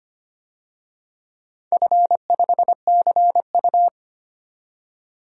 Morse code, heard as a keyed tone, about 25 wpm, 700 Hz. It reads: F5CU
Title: DXCC-100 Call Signs - 4 Characters - 25wpm